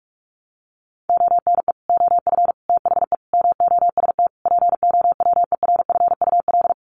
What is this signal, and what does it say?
Morse code, heard as a keyed tone, about 44 wpm, 700 Hz. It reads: ONE OF THE MOST POWERFUL
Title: Sets of 5 Words Encore 45wpm